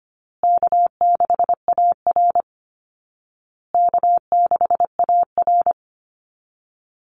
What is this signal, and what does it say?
Morse code, heard as a keyed tone, about 25 wpm, 700 Hz. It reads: K6AR K6AR
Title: Call Signs Repeated 2x - 25wpm